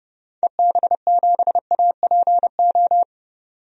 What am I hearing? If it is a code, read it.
Morse code, heard as a keyed tone, about 31 wpm, 690 Hz. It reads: EB7APO
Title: DXCC-100 Call Signs - 6 Characters - 30wpm